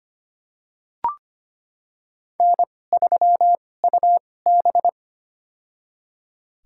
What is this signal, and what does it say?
Morse code, heard as a keyed tone, about 25 wpm, spaced as for 18 wpm, 700 Hz. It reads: N3UB
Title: USA Call Signs - 4 Characters - 2x Character Spacing - 25wpm